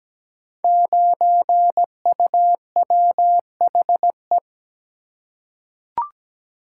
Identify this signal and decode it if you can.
Morse code, heard as a keyed tone, about 17 wpm, 700 Hz. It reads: 9UWHE
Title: Code Groups - 5 Letter:Numbers - 17wpm